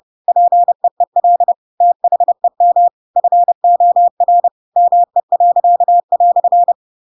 Morse code, heard as a keyed tone, about 30 wpm, 700 Hz. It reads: PEEL THEM FOR ME."
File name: Pinocchio / Pinocchio ch7 30 30 ik1hge